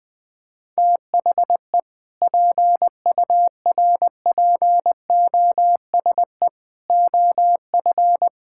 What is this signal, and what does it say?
Morse code, heard as a keyed tone, about 19 wpm, 700 Hz. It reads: THE PURPOSE OF
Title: Sentences with 5-Word Phrases - 20wpm